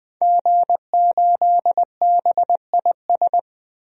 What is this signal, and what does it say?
Morse code, heard as a keyed tone, about 20 wpm, 700 Hz. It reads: G8BIS
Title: DXCC-100 Call Signs - 5 Characters - 20wpm